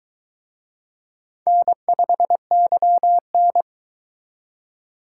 Morse code, heard as a keyed tone, about 24 wpm, 700 Hz. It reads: N5YN